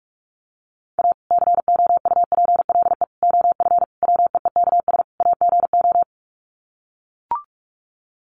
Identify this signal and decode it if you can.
Morse code, heard as a keyed tone, about 44 wpm, 700 Hz. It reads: A COUPLE OF WEEKS AGO
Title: Sets of 5 Words 45wpm